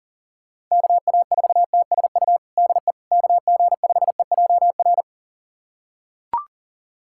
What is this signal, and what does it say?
Morse code, heard as a keyed tone, about 37 wpm, 700 Hz. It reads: KA4TSU DE KG5EJR
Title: USA Callsign DE Callsign - 6 Characters - 40wpm